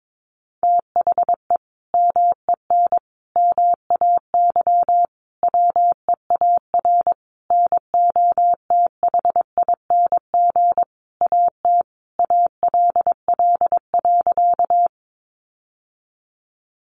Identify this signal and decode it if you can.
Morse code, heard as a keyed tone, about 22 wpm, 700 Hz. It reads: THE MEN MAY WEAR NOTHING AT ALL.
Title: Sentences from Top 1500 Words 22wpm